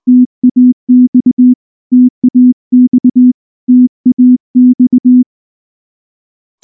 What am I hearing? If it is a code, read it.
Morse code, heard as a keyed tone, about 20 wpm, 250 Hz. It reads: TAX TAX TAX
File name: morse / morse-training-20WPM-4